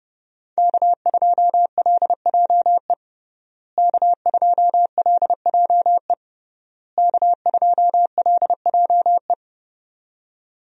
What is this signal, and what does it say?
Morse code, heard as a keyed tone, about 29 wpm, 700 Hz. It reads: K2LJE K2LJE K2LJE
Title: USA Call Signs - 5 Characters - Repeated 3x - 30wpm